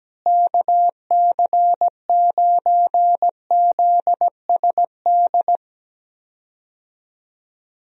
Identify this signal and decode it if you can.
Morse code, heard as a keyed tone, about 17 wpm, 700 Hz. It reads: KC9ZSD
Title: USA Call Signs - 6 Characters - Repeated 3x - 17wpm